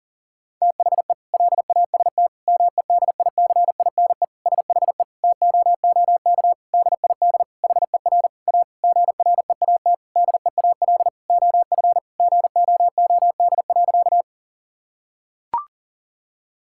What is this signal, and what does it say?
Morse code, heard as a keyed tone, about 37 wpm, 700 Hz. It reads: THE LAST MEDICINE SHE TOOK DID HER A GREAT DEAL OF GOOD.
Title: Sentences from Top 1000 Words - Review Part 6 - 40wpm